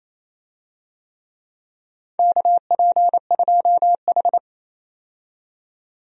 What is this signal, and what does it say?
Morse code, heard as a keyed tone, about 28 wpm, 700 Hz. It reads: KP2H
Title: DXCC-100 Call Signs - 4 Characters - 28wpm